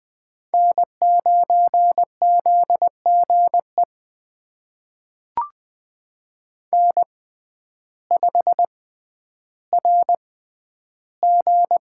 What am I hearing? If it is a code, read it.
Morse code, heard as a keyed tone, about 20 wpm, spaced as for 11 wpm, 700 Hz. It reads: N9ZGE N5RG
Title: USA Call Signs - 6x Character Spacing - 20wpm